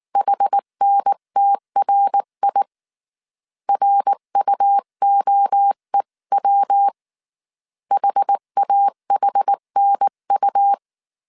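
Morse code, heard as a keyed tone, about 19 wpm, 800 Hz. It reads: HNTRI RUOEW HAHNU
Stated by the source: Morse code dictation at 18 wpm from the IW3SGT QRS Morse code training course, five-character groups of 300 characters.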